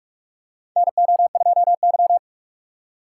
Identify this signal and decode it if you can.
Morse code, heard as a keyed tone, about 46 wpm, 700 Hz. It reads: NO2Y
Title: USA Call Signs - 6x Character Spacing - 45wpm